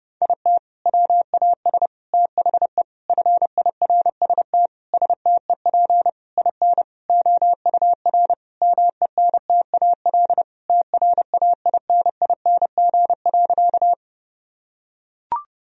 Morse code, heard as a keyed tone, about 30 wpm, 700 Hz. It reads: IT WAS THE FIRST STEP IN OUR MENTAL TRAINING.
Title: Sentences from Top 1000 Words - Review Part 4 - 30wpm